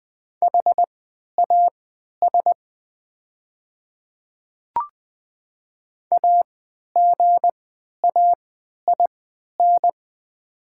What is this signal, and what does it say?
Morse code, heard as a keyed tone, about 20 wpm, spaced as for 11 wpm, 700 Hz. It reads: HAS AGAIN
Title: 3x Character Spacing - Top 200 Words - 20wpm